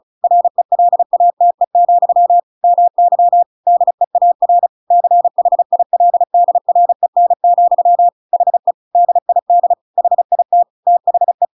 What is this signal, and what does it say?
Morse code, heard as a keyed tone, about 34 wpm, 700 Hz. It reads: RELATE, MY DEAR CHILDREN, HE DID HIT THE